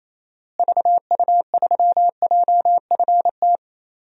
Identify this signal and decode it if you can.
Morse code, heard as a keyed tone, about 28 wpm, 700 Hz. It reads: VU3JFT
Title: DXCC-100 Call Signs - 6 Characters - Repeated 3x - 28wpm